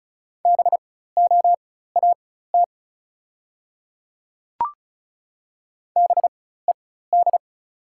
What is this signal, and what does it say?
Morse code, heard as a keyed tone, about 36 wpm, spaced as for 16 wpm, 690 Hz. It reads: BOAT BED
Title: 4x Character Spacing - Top 500 Words - 35wpm